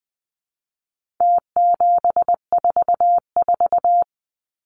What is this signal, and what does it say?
Morse code, heard as a keyed tone, about 20 wpm, 700 Hz. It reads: T744
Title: Numbers - Contest Style - 4-Digit 20wpm